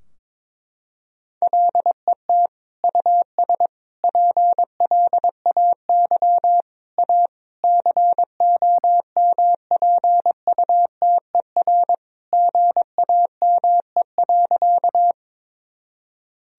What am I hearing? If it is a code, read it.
Morse code, heard as a keyed tone, about 22 wpm, 700 Hz. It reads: LET US PLAY A COMPUTER GAME.
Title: Sentences from Top 1000 Words - Review Part 3 - 22wpm